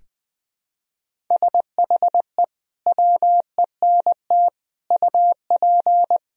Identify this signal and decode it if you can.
Morse code, heard as a keyed tone, about 21 wpm, 700 Hz. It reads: SHE WENT UP
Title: Sentences from Top 1000 Words - Review Part 1 - 20wpm